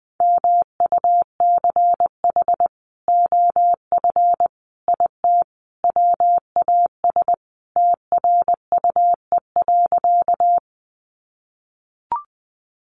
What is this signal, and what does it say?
Morse code, heard as a keyed tone, about 20 wpm, 700 Hz. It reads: MUCH OF IT WAS TRUE.
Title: Sentences from Top 400 Words 20wpm